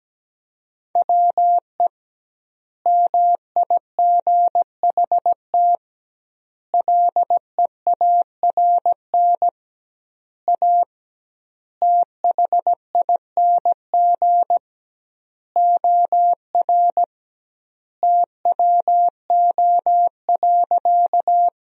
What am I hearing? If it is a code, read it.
Morse code, heard as a keyed tone, about 17 wpm, 700 Hz. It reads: WE MIGHT LEARN A THING OR TWO.
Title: 2x Word-Spacing - Sentences from Top 300 Words - 17wpm